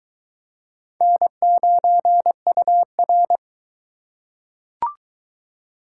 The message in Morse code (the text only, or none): N9UR